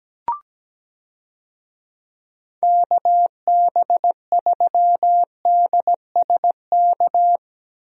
Morse code, heard as a keyed tone, about 17 wpm, 700 Hz. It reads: KB3DSK
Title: USA Call Signs - 6 Characters - Repeated 2x - 17wpm